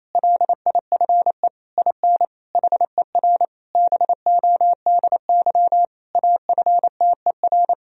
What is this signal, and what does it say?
Morse code, heard as a keyed tone, about 28 wpm, 700 Hz. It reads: LIFE IN HER BODY AFTER
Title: Sentences from Top 1000 Words - Review Part 2 - 28wpm